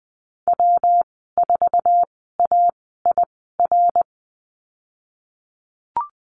Morse code, heard as a keyed tone, about 20 wpm, spaced as for 15 wpm, 700 Hz. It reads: W4AIR
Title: 147 Lesson 24g - Letter y - Call Signs 20wpm